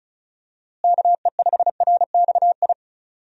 Morse code, heard as a keyed tone, about 36 wpm, 700 Hz. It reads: KE5RXI
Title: USA Call Signs - 6 Characters - Repeated 3x - 35wpm